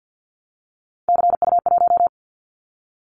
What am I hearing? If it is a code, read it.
Morse code, heard as a keyed tone, about 50 wpm, 700 Hz. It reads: CU1
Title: Callsign Prefixes for Top 50 DXCC - 50wpm